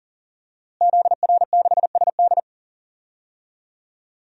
Morse code, heard as a keyed tone, about 40 wpm, 700 Hz. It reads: ZR6SD
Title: DXCC-100 Call Signs - 5 Characters - Repeated 3x - 40wpm